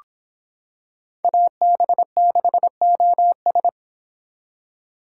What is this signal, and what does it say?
Morse code, heard as a keyed tone, about 26 wpm, 700 Hz. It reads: AB6OS